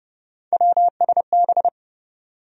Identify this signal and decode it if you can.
Morse code, heard as a keyed tone, about 30 wpm, 700 Hz. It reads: WSB
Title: USA Call Sign Postfixes - 3 Letters 30wpm